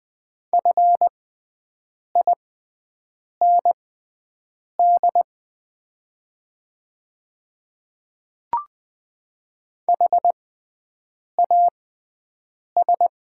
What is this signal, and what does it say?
Morse code, heard as a keyed tone, about 20 wpm, spaced as for 7 wpm, 700 Hz. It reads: FIND HAS
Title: 6x Character Spacing - Top 100 Words - 20wpm